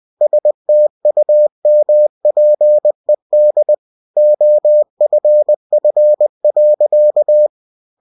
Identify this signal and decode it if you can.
Morse code, heard as a keyed tone, about 20 wpm, 600 Hz. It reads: STUMPED OFF.
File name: cw-books / winnie-the-pooh / winnie-20wpm-page-0077